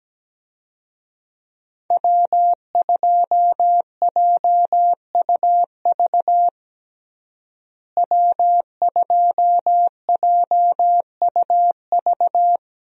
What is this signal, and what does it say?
Morse code, heard as a keyed tone, about 17 wpm, 700 Hz. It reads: W2JUV W2JUV
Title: USA Call Signs - 5 Characters - Repeated 2x - 17wpm